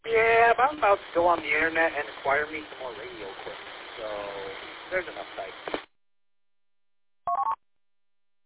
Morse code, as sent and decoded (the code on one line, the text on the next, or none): none